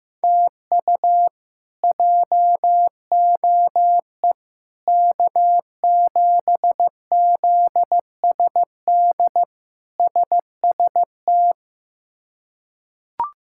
Morse code, heard as a keyed tone, about 15 wpm, 700 Hz. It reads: TU JOE K7ZSD SST
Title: K1USN SST QSO Practice - 15wpm